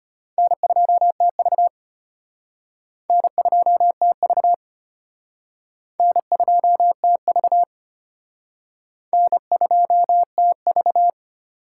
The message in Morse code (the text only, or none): N2TV N2TV N2TV N2TV